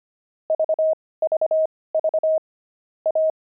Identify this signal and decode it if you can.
Morse code, heard as a keyed tone, about 25 wpm, spaced as for 18 wpm, 630 Hz. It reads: VVV A
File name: pangram-fs-18-cs-25